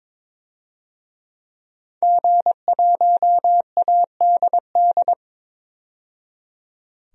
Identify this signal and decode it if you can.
Morse code, heard as a keyed tone, about 22 wpm, 700 Hz. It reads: G1ADD
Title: DXCC-100 Call Signs - 5 Characters - 22wpm